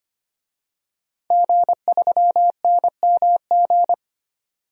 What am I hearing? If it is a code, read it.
Morse code, heard as a keyed tone, about 25 wpm, 700 Hz. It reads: G3NMG
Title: DXCC-100 Call Signs - 5 Characters - 25wpm